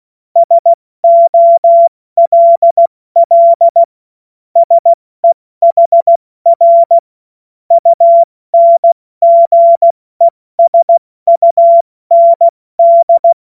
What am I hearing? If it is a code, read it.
Morse code, heard as a keyed tone, about 16 wpm, 680 Hz. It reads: SOLL SEHR UNGESUND
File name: Witz-02-September-2025-70-F80